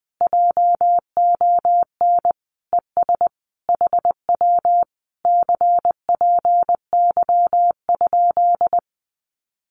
Morse code, heard as a keyed tone, about 20 wpm, 700 Hz. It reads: JON ES HW CPY?
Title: Common QSO Phrases Encore 20wpm